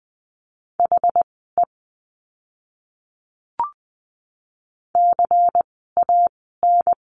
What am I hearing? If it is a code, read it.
Morse code, heard as a keyed tone, about 20 wpm, spaced as for 14 wpm, 700 Hz. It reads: HE CAN
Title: 244 Lesson 39f - Letter z - Words 20wpm